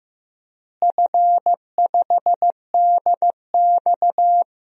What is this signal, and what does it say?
Morse code, heard as a keyed tone, about 15 wpm, 700 Hz. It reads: F5DX
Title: DXCC-100 Call Signs - 4 Characters - Repeated 3x - 15wpm